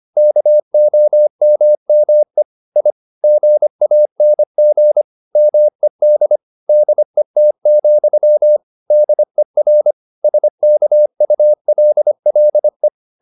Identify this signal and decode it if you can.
Morse code, heard as a keyed tone, about 25 wpm, 600 Hz. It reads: KOMME I GANG MED DET, DER SKULLE
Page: Ferie 24 wpm